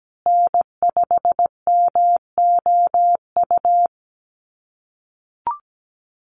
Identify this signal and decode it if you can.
Morse code, heard as a keyed tone, about 17 wpm, 700 Hz. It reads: N5MOU
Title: USA Call Signs Repeated 3x 17wpm